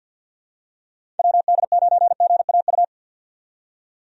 Morse code, heard as a keyed tone, about 52 wpm, 700 Hz. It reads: WD9GAU
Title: USA Call Signs - 6 Characters - 50wpm